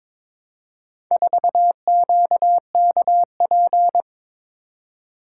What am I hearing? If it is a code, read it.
Morse code, heard as a keyed tone, about 22 wpm, 700 Hz. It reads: 4QKP
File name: practice / 22wpm / Four Letter-Number Combinations - 22wpm